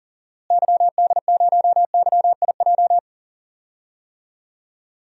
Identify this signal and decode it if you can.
Morse code, heard as a keyed tone, about 40 wpm, 700 Hz. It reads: YD0YIJ
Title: DXCC-100 Call Signs - 6 Characters - Repeated 3x - 40wpm